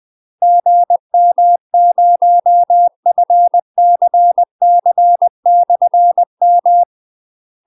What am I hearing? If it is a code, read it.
Morse code, heard as a keyed tone, about 20 wpm, 700 Hz. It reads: GM0FCC/M
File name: UK MASTER QSOs TRAINING / 20WPM / UK MASTER QSOs 0050